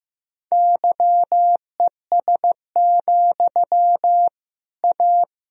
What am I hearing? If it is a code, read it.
Morse code, heard as a keyed tone, about 15 wpm, 700 Hz. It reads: YES, A
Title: Sentences from Top 1000 Words - Review Part 3 - 15wpm